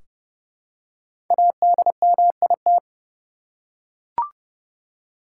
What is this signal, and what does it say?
Morse code, heard as a keyed tone, about 32 wpm, 700 Hz. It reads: ADMIT
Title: Five Letter Words - 30wpm